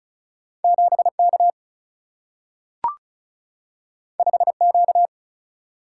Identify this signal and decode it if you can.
Morse code, heard as a keyed tone, about 36 wpm, 700 Hz. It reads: 7K 5Q
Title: Single-Letter Single-Number 35wpm